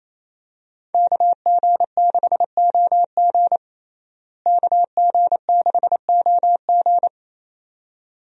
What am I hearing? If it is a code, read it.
Morse code, heard as a keyed tone, about 28 wpm, 700 Hz. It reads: KG6OG KG6OG
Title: USA Call Signs - 5 Characters - Repeated 2x - 28wpm